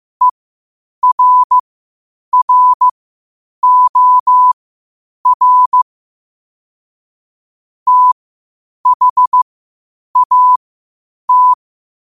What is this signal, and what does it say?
Morse code, heard as a keyed tone, about 15 wpm, spaced as for 8 wpm, 1000 Hz. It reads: ERROR THAT
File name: qotdfiles / 8wpm